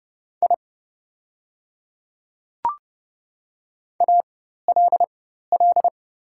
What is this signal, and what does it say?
Morse code, heard as a keyed tone, about 30 wpm, spaced as for 14 wpm, 700 Hz. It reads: I ALL
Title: 4x Character Spacing - Top 100 Words - 30wpm